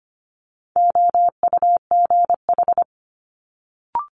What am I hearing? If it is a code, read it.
Morse code, heard as a keyed tone, about 25 wpm, 700 Hz. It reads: OUGH
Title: Multi-letter Phonemes Quick Fire 25wpm